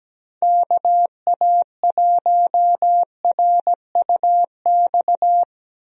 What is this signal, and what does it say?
Morse code, heard as a keyed tone, about 17 wpm, 700 Hz. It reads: KA1RUX
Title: USA Call Signs - 6 Characters - 17wpm